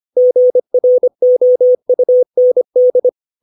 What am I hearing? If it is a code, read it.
Morse code, heard as a keyed tone, about 25 wpm, 500 Hz. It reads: GROUND
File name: 632 words / 632 Words 25WPM